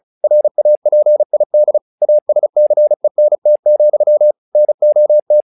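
Morse code, heard as a keyed tone, about 36 wpm, 600 Hz. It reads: RAPID ASCENT, NOT
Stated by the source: Morse practice text "Tower Trouble" sent at 35 WPM, from the SKCC - Straight Key Century Club learning center audio files.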